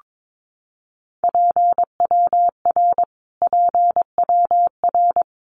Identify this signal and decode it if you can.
Morse code, heard as a keyed tone, about 22 wpm, 700 Hz. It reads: PWR PWR
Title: Abbreviations Repeated 3x - 22wpm